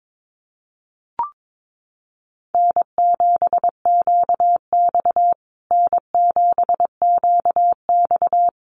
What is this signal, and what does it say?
Morse code, heard as a keyed tone, about 22 wpm, 700 Hz. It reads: N7QX N7QX
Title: USA Call Signs Repeated 3x 22wpm